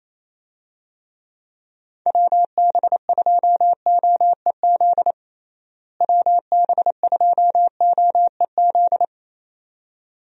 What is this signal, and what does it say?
Morse code, heard as a keyed tone, about 28 wpm, 700 Hz. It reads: WB2OEZ WB2OEZ
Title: USA Call Signs - 6 Characters - Repeated 2x - 28wpm